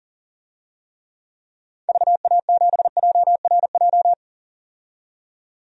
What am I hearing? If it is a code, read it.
Morse code, heard as a keyed tone, about 38 wpm, 700 Hz. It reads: VA7JRJ
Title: DXCC-100 Call Signs - 6 Characters - 40wpm